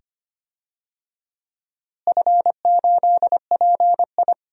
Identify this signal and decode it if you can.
Morse code, heard as a keyed tone, about 25 wpm, 700 Hz. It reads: F8PI
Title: DXCC-100 Call Signs - 4 Characters - Repeated 2x - 25wpm